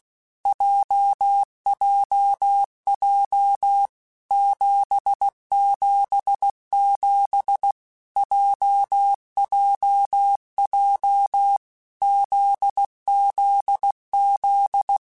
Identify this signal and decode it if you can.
Morse code, heard as a keyed tone, about 16 wpm, 800 Hz. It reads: JJJ 777 JJJ ZZZ
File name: Basic Letter Groups By7Groups / JZ17 / JZ17@15